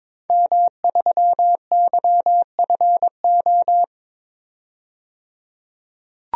Morse code, heard as a keyed tone, about 22 wpm, 700 Hz. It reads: M3YFO
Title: DXCC-100 Call Signs - 5 Characters - 22wpm